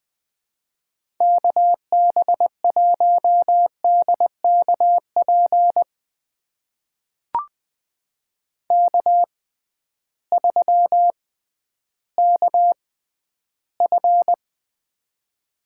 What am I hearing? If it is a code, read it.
Morse code, heard as a keyed tone, about 20 wpm, spaced as for 12 wpm, 700 Hz. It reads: KB1DKP K3KF
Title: USA Call Signs - 6x Character Spacing - 20wpm